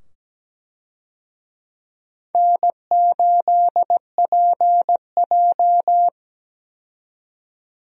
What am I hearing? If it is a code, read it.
Morse code, heard as a keyed tone, about 17 wpm, 700 Hz. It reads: N8PJ
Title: USA Call Signs - 4 Characters - 17wpm